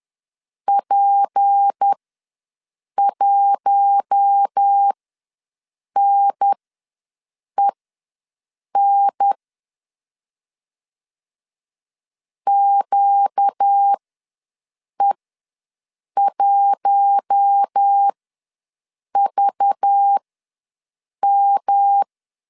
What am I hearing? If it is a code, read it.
Morse code, heard as a keyed tone, about 10 wpm, spaced as for 6 wpm, 800 Hz. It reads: P1NEN QE1VM
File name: LEZIONI / 6 PKX0 / 6 P ws 10wpm 3751